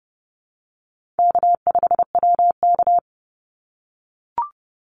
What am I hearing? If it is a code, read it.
Morse code, heard as a keyed tone, about 30 wpm, 700 Hz. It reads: K5WK
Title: Easy Call Signs - 30wpm